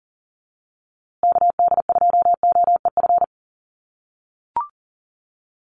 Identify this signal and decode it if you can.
Morse code, heard as a keyed tone, about 40 wpm, 700 Hz. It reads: KD2OEF
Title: USA Call Signs Repeated 3x 40wpm